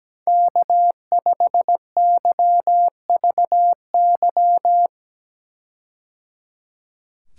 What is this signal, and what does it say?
Morse code, heard as a keyed tone, about 17 wpm, 700 Hz. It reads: K5YVY
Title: K1USN SST QSO Practice - 17wpm